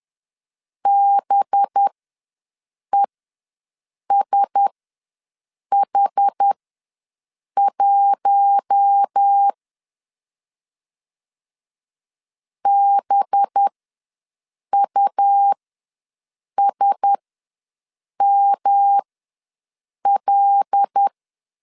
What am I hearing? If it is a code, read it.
Morse code, heard as a keyed tone, about 10 wpm, spaced as for 6 wpm, 800 Hz. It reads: BESH1 BUSML
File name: LEZIONI / 5 LBVC / 5 LB ws 10wpm 3616